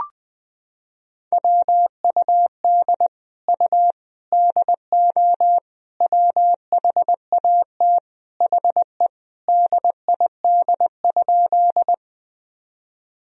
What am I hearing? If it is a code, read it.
Morse code, heard as a keyed tone, about 21 wpm, 700 Hz. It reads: WUD U DO WHAT HE DID?
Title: Sentences from Top 100 Words and Abbreviations 20wpm